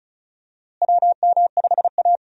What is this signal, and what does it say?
Morse code, heard as a keyed tone, about 36 wpm, 700 Hz. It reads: WM5A
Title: USA Call Signs - 4 Characters - Repeated 3x - 35wpm